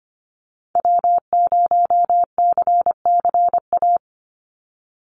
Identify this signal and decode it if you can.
Morse code, heard as a keyed tone, about 25 wpm, 700 Hz. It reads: W0CCA
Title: NA SOTA Activator Callsigns 25wpm